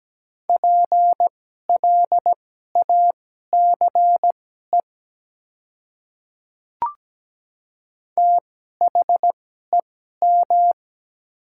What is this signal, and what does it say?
Morse code, heard as a keyed tone, about 17 wpm, spaced as for 12 wpm, 700 Hz. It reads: PLACE THEM
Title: 2x Character Spacing - Top 200 Words - 17wpm